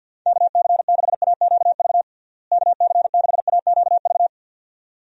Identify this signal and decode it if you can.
Morse code, heard as a keyed tone, about 51 wpm, 700 Hz. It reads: KC6AQV KC6AQV
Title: USA Call Signs - 6 Characters - Repeated 2x - 50wpm